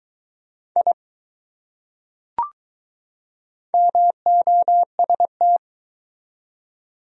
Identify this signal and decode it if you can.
Morse code, heard as a keyed tone, about 20 wpm, 700 Hz. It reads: I MOST